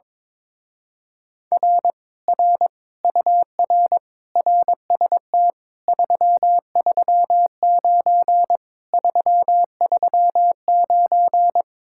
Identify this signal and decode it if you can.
Morse code, heard as a keyed tone, about 22 wpm, 700 Hz. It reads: R R UR RST 339 339
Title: Speed-Racing Common QSO Phrases - 22wpm